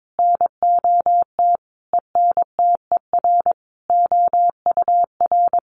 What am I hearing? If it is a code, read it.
Morse code, heard as a keyed tone, about 22 wpm, 700 Hz. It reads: NOT ENTER OUR
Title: Sentences from Top 1300 Words 22wpm